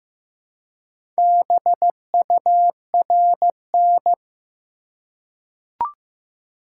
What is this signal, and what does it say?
Morse code, heard as a keyed tone, about 15 wpm, 700 Hz. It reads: BURN
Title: Four Letter Words - 15wpm